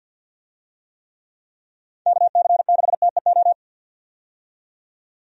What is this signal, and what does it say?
Morse code, heard as a keyed tone, about 48 wpm, 700 Hz. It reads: KC6TEO
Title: USA Call Signs - 6 Characters - 50wpm